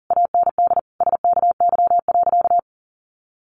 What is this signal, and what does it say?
Morse code, heard as a keyed tone, about 40 wpm, 700 Hz. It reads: AND SKY.
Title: Sentences from Top 1200 Words 40wpm